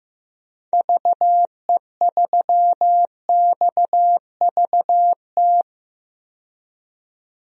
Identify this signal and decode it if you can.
Morse code, heard as a keyed tone, about 15 wpm, 700 Hz. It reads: VE3XVT
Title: DXCC-100 Call Signs - 6 Characters - 15wpm